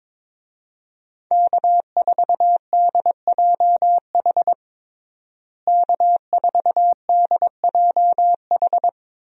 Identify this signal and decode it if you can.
Morse code, heard as a keyed tone, about 22 wpm, 700 Hz. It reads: K4DJH K4DJH
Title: USA Call Signs - 5 Characters - Repeated 2x - 22wpm